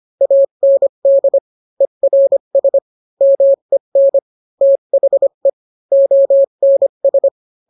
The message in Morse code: AND ERS MEN THE ONS